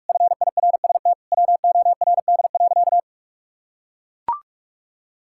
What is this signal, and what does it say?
Morse code, heard as a keyed tone, about 41 wpm, 700 Hz. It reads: FIRST WORD.
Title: Pangram Challenge v2 - 45wpm